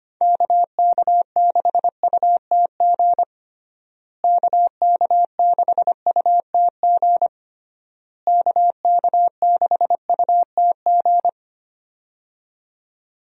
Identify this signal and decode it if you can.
Morse code, heard as a keyed tone, about 25 wpm, 700 Hz. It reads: KK6UTG KK6UTG KK6UTG
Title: USA Call Signs - 6 Characters - Repeated 3x - 25wpm